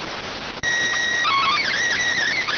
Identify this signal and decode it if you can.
SSTV (slow-scan television), PD 290.